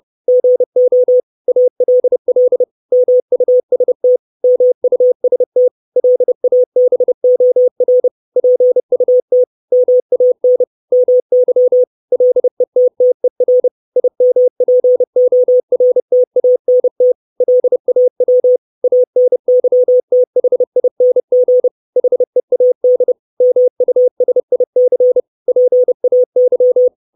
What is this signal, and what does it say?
Morse code, heard as a keyed tone, about 30 wpm, 500 Hz. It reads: GO ALL MUST MUST LABOR PUT MAN MY LETTER IMPORTANT LAW ANYTHING HEAD MUSIC PAY
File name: 632 Words 30WPM